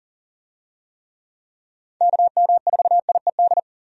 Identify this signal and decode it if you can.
Morse code, heard as a keyed tone, about 40 wpm, 700 Hz. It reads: KM4IED
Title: USA Call Signs - 6 Characters - 40wpm